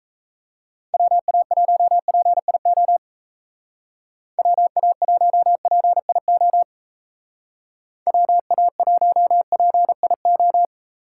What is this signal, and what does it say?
Morse code, heard as a keyed tone, about 36 wpm, 700 Hz. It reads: WA1PIO WA1PIO WA1PIO